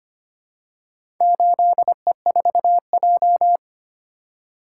Morse code, heard as a keyed tone, about 25 wpm, 700 Hz. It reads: 8E4J
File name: practice / 25wpm / Four Letter-Number Combinations - 25wpm